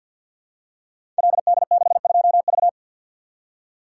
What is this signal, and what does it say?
Morse code, heard as a keyed tone, about 52 wpm, 700 Hz. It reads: LD62V
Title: Code Groups - 5 Letter:Numbers - 50wpm